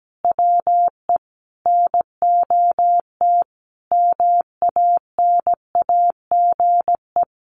Morse code, heard as a keyed tone, about 17 wpm, 700 Hz. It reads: WE NOT MANAGE
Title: Sentences from Top 1400 Words 17wpm